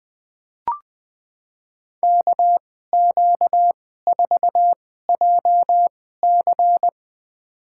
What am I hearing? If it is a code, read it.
Morse code, heard as a keyed tone, about 20 wpm, spaced as for 14 wpm, 700 Hz. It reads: KQ4JC
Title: USA Call Signs - 2x Character Spacing - 20wpm